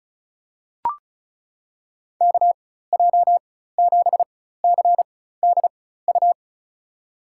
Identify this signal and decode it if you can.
Morse code, heard as a keyed tone, about 36 wpm, spaced as for 16 wpm, 700 Hz. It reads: KJ7CDU
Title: USA Call Signs - 4x Character Spacing - 35wpm